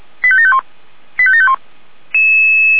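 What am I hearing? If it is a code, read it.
Five-tone selective call (ZVEI), heard as quick runs of five tones, tones of about 70 ms. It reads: 85751 85751
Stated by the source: Five-tone ZVEI selective call 89485, received on an SDR.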